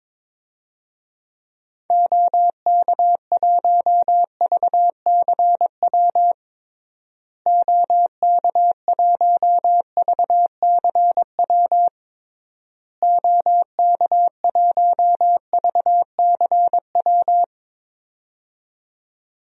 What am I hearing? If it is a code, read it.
Morse code, heard as a keyed tone, about 21 wpm, 700 Hz. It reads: OK1VCW OK1VCW OK1VCW
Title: DXCC-100 Call Signs - 6 Characters - Repeated 3x - 22wpm